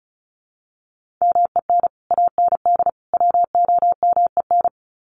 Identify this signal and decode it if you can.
Morse code, heard as a keyed tone, about 34 wpm, 700 Hz. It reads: MEN AND WOMEN
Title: Sets of 3 Words 35wpm